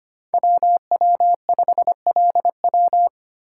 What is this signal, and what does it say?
Morse code, heard as a keyed tone, about 25 wpm, 700 Hz. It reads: WW5LW
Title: USA Call Signs - 5 Characters - 2x Character Spacing - 25wpm